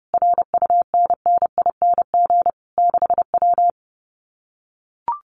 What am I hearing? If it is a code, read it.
Morse code, heard as a keyed tone, about 30 wpm, 700 Hz. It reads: RUNNING 6W
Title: Common QSO Phrases Encore 30wpm